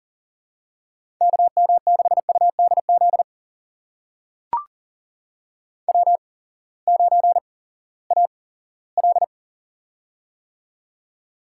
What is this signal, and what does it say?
Morse code, heard as a keyed tone, about 38 wpm, spaced as for 20 wpm, 700 Hz. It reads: KM6UDZ W9AL
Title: USA Call Signs - 8x Character Spacing - 40wpm